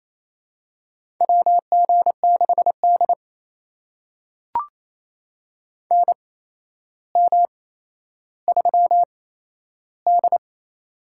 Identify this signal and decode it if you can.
Morse code, heard as a keyed tone, about 28 wpm, spaced as for 12 wpm, 700 Hz. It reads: WG6D NM3D
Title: USA Call Signs - 4 Characters - 8x Character Spacing - 28wpm